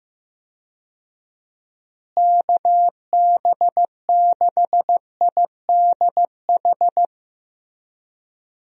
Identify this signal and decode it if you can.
Morse code, heard as a keyed tone, about 15 wpm, 700 Hz. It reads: KB6IDH
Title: USA Call Signs - 6 Characters - Repeated 2x - 15wpm